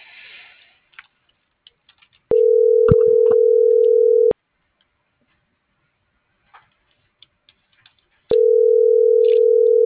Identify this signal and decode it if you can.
Ringback tone (North America), 440 + 480 Hz.